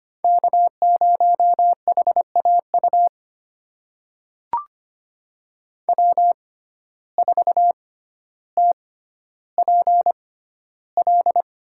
Morse code, heard as a keyed tone, about 25 wpm, spaced as for 13 wpm, 700 Hz. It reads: K0HAU W4TPL
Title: USA Call Signs - 5 Characters - 6x Character Spacing - 25wpm